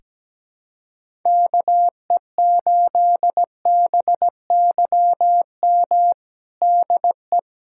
Morse code, heard as a keyed tone, about 17 wpm, 700 Hz. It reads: KE8BYM DE
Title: USA Callsign DE Callsign - 6 Characters - 17wpm